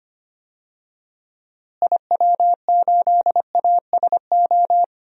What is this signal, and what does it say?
Morse code, heard as a keyed tone, about 25 wpm, 700 Hz. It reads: IW8ASO
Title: DXCC-100 Call Signs - 6 Characters - Repeated 2x - 25wpm